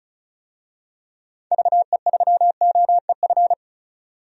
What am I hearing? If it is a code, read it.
Morse code, heard as a keyed tone, about 36 wpm, 700 Hz. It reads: VE3OEF
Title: DXCC-100 Call Signs - 6 Characters - 35wpm